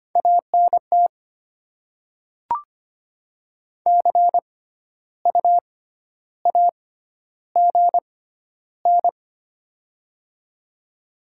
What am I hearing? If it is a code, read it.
Morse code, heard as a keyed tone, about 25 wpm, spaced as for 12 wpm, 700 Hz. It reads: ANT CU AGN
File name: Top 20 QSO Elements - 6x Character Spacing - 25wpm